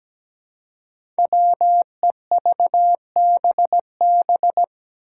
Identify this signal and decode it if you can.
Morse code, heard as a keyed tone, about 17 wpm, 700 Hz. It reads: WEVBB
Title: Five Challenging Letter-Number Combinations 17wpm